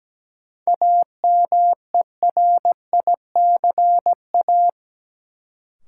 Morse code, heard as a keyed tone, about 17 wpm, 700 Hz. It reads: AMERICA
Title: Sets of 5 Words - New Word Drill - 17wpm